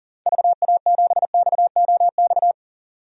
Morse code, heard as a keyed tone, about 39 wpm, 700 Hz. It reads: VA7XOX
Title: DXCC-100 Call Signs - 6 Characters - 40wpm